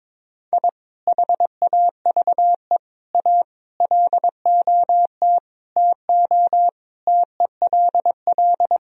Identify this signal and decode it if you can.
Morse code, heard as a keyed tone, about 22 wpm, 700 Hz. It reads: I HAVE A LOT TO TELL
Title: Sentences from Top 1000 Words - Review Part 3 - 22wpm